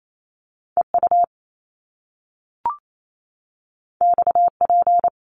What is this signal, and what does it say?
Morse code, heard as a keyed tone, about 28 wpm, 700 Hz. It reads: EU XP
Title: Top Two Letter Combinations in Words 28wpm